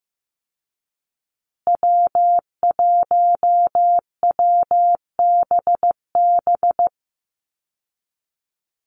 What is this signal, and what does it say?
Morse code, heard as a keyed tone, about 15 wpm, 700 Hz. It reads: W1WBB
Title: Call Signs Repeated 2x - 15wpm